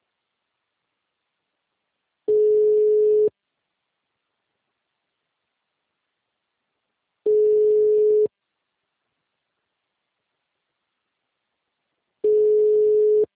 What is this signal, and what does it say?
Ringback tone (Europe, 425 Hz).
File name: out- -2013- - - .1134